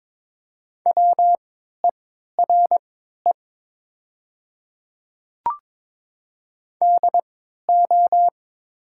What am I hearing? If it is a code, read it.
Morse code, heard as a keyed tone, about 22 wpm, spaced as for 12 wpm, 700 Hz. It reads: WERE DO
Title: 3x Character Spacing - Top 200 Words - 22wpm